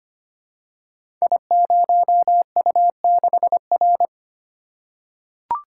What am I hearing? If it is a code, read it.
Morse code, heard as a keyed tone, about 25 wpm, 700 Hz. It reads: I0U6R
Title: Code Groups - 5 Letter:Numbers - 25wpm